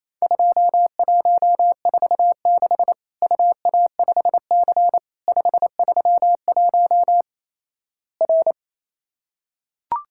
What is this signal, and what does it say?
Morse code, heard as a keyed tone, about 28 wpm, 700 Hz. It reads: 2146 UA5C 531 R
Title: Worked All Europe DX Contest - QTC Exchange - 28wpm